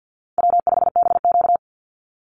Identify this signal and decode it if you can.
Morse code, heard as a keyed tone, about 50 wpm, 700 Hz. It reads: W5BQ
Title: Easy Call Signs Repeated 2x - 50wpm